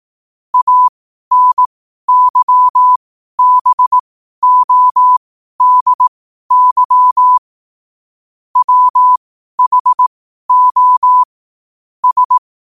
Morse code, heard as a keyed tone, about 18 wpm, spaced as for 12 wpm, 1000 Hz. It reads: ANYBODY WHO S
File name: qotdfiles / 12wpm